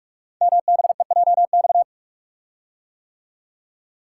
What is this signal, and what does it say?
Morse code, heard as a keyed tone, about 44 wpm, 700 Hz. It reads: MBEJX
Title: Code Groups - 5 Letters - 45wpm